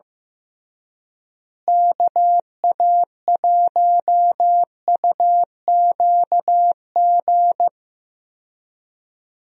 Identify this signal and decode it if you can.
Morse code, heard as a keyed tone, about 15 wpm, 700 Hz. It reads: KA1UQG
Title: USA Call Signs - 6 Characters - Repeated 3x - 15wpm